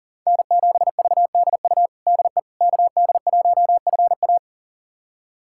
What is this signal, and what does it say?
Morse code, heard as a keyed tone, about 37 wpm, 700 Hz. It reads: N7VDU DE KD1FA
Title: USA Callsign DE Callsign - 5 Characters - 40wpm